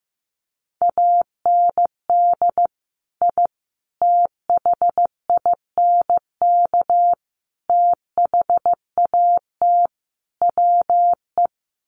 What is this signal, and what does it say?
Morse code, heard as a keyed tone, about 15 wpm, 700 Hz. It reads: AND I THINK THAT WE
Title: Sets of 5 Words Encore2 15wpm